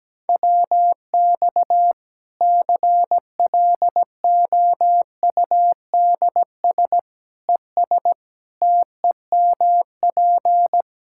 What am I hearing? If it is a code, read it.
Morse code, heard as a keyed tone, about 17 wpm, 700 Hz. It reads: WX CLOUDS ES TEMP
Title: K1USN SST QSO Practice 17wpm